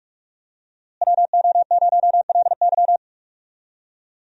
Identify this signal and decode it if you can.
Morse code, heard as a keyed tone, about 46 wpm, 700 Hz. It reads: WO0LY
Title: USA Call Signs - 5 Characters - Repeated 3x - 45wpm